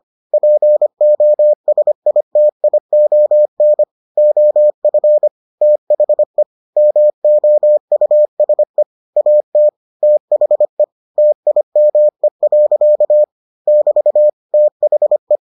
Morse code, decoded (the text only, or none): POSITION OF THE MOUSE AT THE TIME. <BT> THE